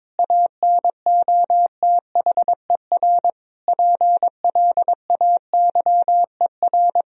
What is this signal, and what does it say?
Morse code, heard as a keyed tone, about 22 wpm, 700 Hz. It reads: ANOTHER PLAYER
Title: Sentences from Top 1000 Words - Review Part 4 - 22wpm